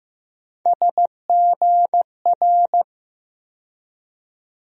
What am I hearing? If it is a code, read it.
Morse code, heard as a keyed tone, about 15 wpm, 700 Hz. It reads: SGR
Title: USA Call Sign Postfixes - 3 Letters 15wpm